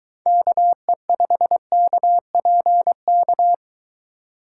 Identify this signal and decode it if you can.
Morse code, heard as a keyed tone, about 22 wpm, 700 Hz. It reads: KE5KPK